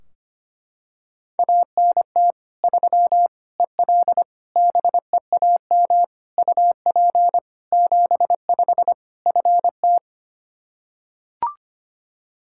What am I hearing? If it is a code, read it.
Morse code, heard as a keyed tone, about 25 wpm, 700 Hz. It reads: ANT 3 EL BEAM UP 75 FT
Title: Practice QSO - Exchange 2 - Info - 25wpm